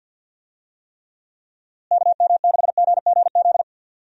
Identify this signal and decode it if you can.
Morse code, heard as a keyed tone, about 46 wpm, 700 Hz. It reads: KM6GGZ
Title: USA Call Signs - 6 Characters - Repeated 2x - 50wpm